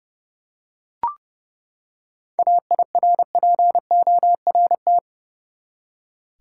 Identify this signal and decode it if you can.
Morse code, heard as a keyed tone, about 30 wpm, 700 Hz. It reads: AIRPORT
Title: Seven Letter Words - 30wpm